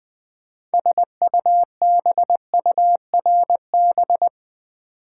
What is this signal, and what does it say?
Morse code, heard as a keyed tone, about 20 wpm, 700 Hz. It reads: SUBURB
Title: Build A Word - 3 Letters At A Time - 6-Letter Words - 20wpm